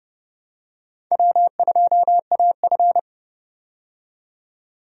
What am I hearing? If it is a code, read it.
Morse code, heard as a keyed tone, about 30 wpm, 700 Hz. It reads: W2AF
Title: USA Call Signs - 4 Characters - 30wpm